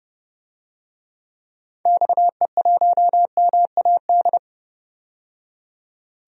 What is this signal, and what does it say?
Morse code, heard as a keyed tone, about 30 wpm, 700 Hz. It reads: XE1MAD
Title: DXCC-100 Call Signs - 6 Characters - 30wpm